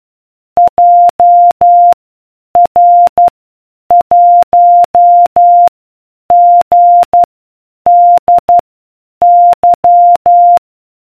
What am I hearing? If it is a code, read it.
Morse code, heard as a keyed tone, about 12 wpm, spaced as for 8 wpm, 700 Hz. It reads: JR1GDY